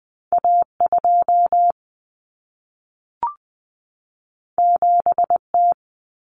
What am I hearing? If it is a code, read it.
Morse code, heard as a keyed tone, about 20 wpm, 700 Hz. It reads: A2 7T
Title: Common Cut Numbers - 2-Digit 20wpm